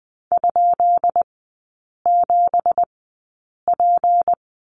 Morse code, heard as a keyed tone, about 20 wpm, spaced as for 8 wpm, 700 Hz. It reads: ?7P
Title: 212 Lesson 34e - Number 8 - Three Characters 20wpm